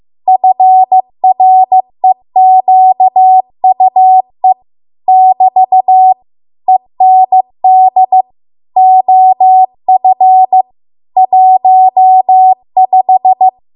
Morse code, heard as a keyed tone, about 15 wpm, 750 Hz. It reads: FREQUE <BT> END OF 15
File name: Archive / 15 WPM / 15WPM